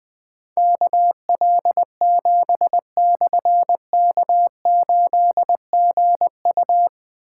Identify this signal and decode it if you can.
Morse code, heard as a keyed tone, about 20 wpm, 700 Hz. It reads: KL7/K8GU
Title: Difficult Call Signs 20wpm